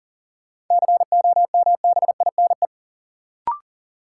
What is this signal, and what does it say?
Morse code, heard as a keyed tone, about 40 wpm, 700 Hz. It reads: COMBINE
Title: Sentences from Top 1600 Words - New Word Drill - 40wpm